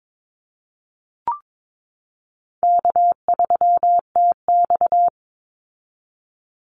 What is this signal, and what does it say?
Morse code, heard as a keyed tone, about 22 wpm, 700 Hz. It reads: K3TX
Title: Easy Call Signs Repeated 3x - 22wpm